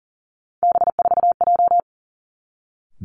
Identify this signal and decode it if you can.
Morse code, heard as a keyed tone, about 42 wpm, 700 Hz. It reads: B4J
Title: Three Challenging Letter-Number Combinations 40wpm